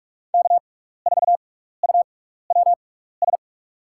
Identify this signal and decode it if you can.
Morse code, heard as a keyed tone, about 44 wpm, spaced as for 15 wpm, 690 Hz. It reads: K4UWS
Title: USA Call Signs - 5 Characters - 6x Character Spacing - 45wpm